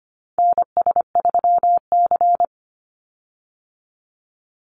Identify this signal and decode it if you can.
Morse code, heard as a keyed tone, about 25 wpm, 700 Hz. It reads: NS3C
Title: Call Signs Repeated 2x - 25wpm